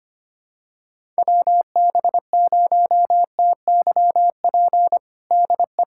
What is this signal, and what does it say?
Morse code, heard as a keyed tone, about 25 wpm, 700 Hz. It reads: WB0TYP DE
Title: USA Callsign DE Callsign - 6 Characters - 25wpm